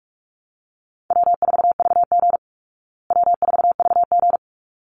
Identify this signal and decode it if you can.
Morse code, heard as a keyed tone, about 44 wpm, 700 Hz. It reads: W4VG W4VG
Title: Call Signs Repeated 3x - 45wpm